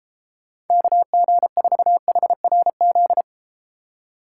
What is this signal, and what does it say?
Morse code, heard as a keyed tone, about 33 wpm, 700 Hz. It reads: KG4HRZ